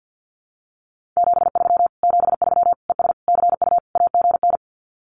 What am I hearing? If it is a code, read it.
Morse code, heard as a keyed tone, about 50 wpm, 700 Hz. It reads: 73 73 ES CU AGN
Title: Common QSO Phrases Encore 50wpm